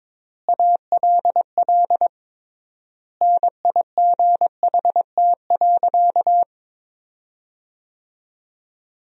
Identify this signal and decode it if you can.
Morse code, heard as a keyed tone, about 22 wpm, 700 Hz. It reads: ALL NIGHT.
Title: 3x Word-Spacing - Sentences from Top 300 Words - 22wpm